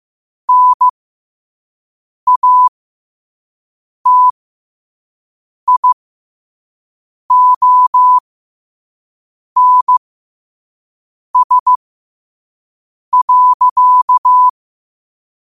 Morse code, heard as a keyed tone, about 15 wpm, spaced as for 5 wpm, 1000 Hz. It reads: NATIONS.
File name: qotdfiles / 5wpm